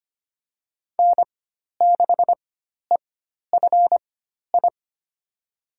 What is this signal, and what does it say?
Morse code, heard as a keyed tone, about 25 wpm, spaced as for 12 wpm, 700 Hz. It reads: N6EFI
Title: USA Call Signs - 5 Characters - 4x Character Spacing - 25wpm